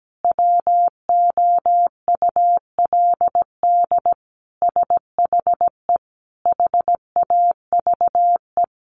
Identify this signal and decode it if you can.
Morse code, heard as a keyed tone, about 17 wpm, 700 Hz. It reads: WOULD SHE HAVE
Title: Sentences from Top 100 Words 17wpm